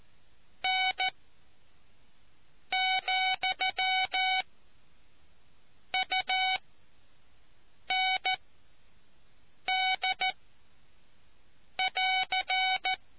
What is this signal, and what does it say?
Morse code, heard as a keyed tone, about 14 wpm, spaced as for 5 wpm, 750 Hz. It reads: N, UND <AR>